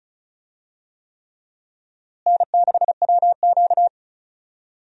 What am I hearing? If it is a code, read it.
Morse code, heard as a keyed tone, about 36 wpm, 700 Hz. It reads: N6WQ
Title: USA Call Signs - 4 Characters - Repeated 3x - 35wpm